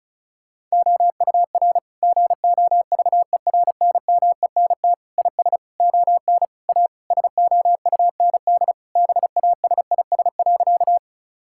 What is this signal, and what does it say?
Morse code, heard as a keyed tone, about 36 wpm, 700 Hz. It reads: OUR GOVERNMENT IS ON A SOUND BASIS.
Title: Sentences from Top 1000 Words - Review Part 5 - 35wpm